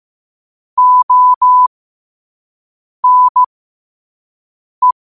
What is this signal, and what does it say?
Morse code, heard as a keyed tone, about 15 wpm, spaced as for 5 wpm, 1000 Hz. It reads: ONE
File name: qotdfiles / 5wpm